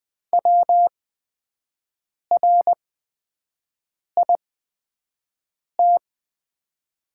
Morse code, heard as a keyed tone, about 20 wpm, spaced as for 5 wpm, 700 Hz. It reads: WRIT
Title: 8x Character Spacing - Top 100 Words - 20wpm